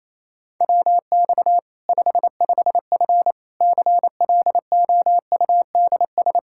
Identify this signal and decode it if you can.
Morse code, heard as a keyed tone, about 28 wpm, 700 Hz. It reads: WX 55F CLOUDS
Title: Practice QSO - Exchange 2 - Info - 28wpm